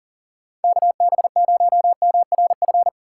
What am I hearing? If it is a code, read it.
Morse code, heard as a keyed tone, about 40 wpm, 700 Hz. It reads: KB0MRF
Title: USA Call Signs - 6 Characters - 3x Character Spacing - 40wpm